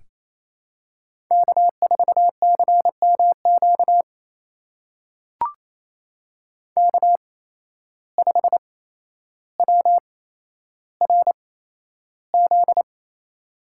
Morse code, heard as a keyed tone, about 27 wpm, spaced as for 12 wpm, 700 Hz. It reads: K4CMQ K5WRZ
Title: USA Call Signs - 5 Characters - 8x Character Spacing - 28wpm